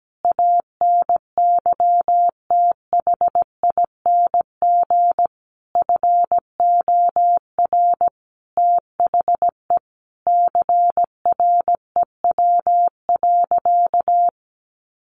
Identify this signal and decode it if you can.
Morse code, heard as a keyed tone, about 17 wpm, 700 Hz. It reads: ANYTHING FOR THE CREW.
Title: Sentences from Top 1500 Words 17wpm